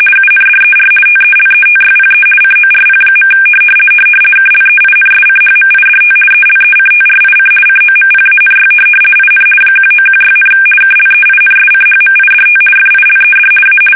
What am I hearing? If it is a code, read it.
RTTY, about 50 baud. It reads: UNDER TELETYPE CORP. KLEINSCHMIDT ELECTRIC COMPANY IN 1916, KLEINSCHMIDT FILED A
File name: fsk850-50B / teletype-corp